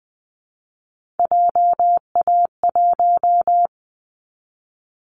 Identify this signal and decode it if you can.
Morse code, heard as a keyed tone, about 20 wpm, 700 Hz. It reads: JA1
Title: Callsign Prefixes for Top 50 DXCC - 20wpm